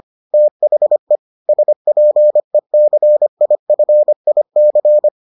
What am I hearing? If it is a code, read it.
Morse code, heard as a keyed tone, about 25 wpm, 600 Hz. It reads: THE SPECIFIC